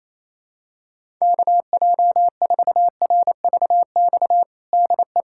Morse code, heard as a keyed tone, about 27 wpm, 700 Hz. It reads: KJ4RVX DE
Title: USA Callsign DE Callsign - 6 Characters - 28wpm